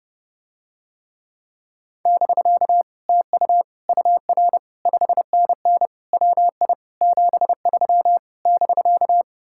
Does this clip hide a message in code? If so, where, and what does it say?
Yes: Morse code from 2.1 to 9.2 s, heard as a keyed tone, about 30 wpm, 700 Hz. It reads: <BK> TU UR 5NN WI 73 <BK>